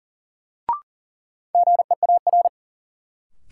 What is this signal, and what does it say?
Morse code, heard as a keyed tone, about 40 wpm, 700 Hz. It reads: GEAR
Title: Sentences from Top 1400 Words - New Word Drill Alt - 40wpm